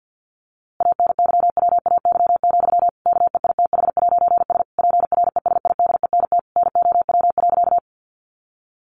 Morse code, heard as a keyed tone, about 50 wpm, 700 Hz. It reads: ANYWAY, KEITH'S PRESIDENT NOW.
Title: A Sound Of Thunder 50wpm